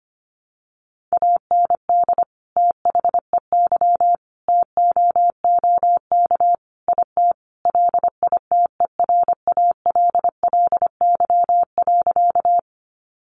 Morse code, heard as a keyed tone, about 25 wpm, 700 Hz. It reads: AND THEY TOOK IT LITERALLY.
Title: Sentences from Top 1400 Words 25wpm